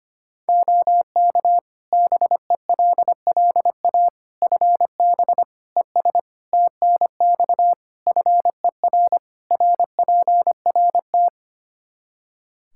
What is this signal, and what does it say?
Morse code, heard as a keyed tone, about 25 wpm, 700 Hz. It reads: OK BELLA FB ES TNX FER RPRT
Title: K1USN SST QSO Practice 25wpm